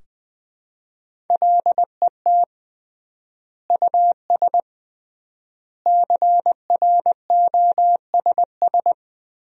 Morse code, heard as a keyed tone, about 21 wpm, 700 Hz. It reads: LET US CROSS
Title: 3x Word-Spacing - Sentences from Top 300 Words - 20wpm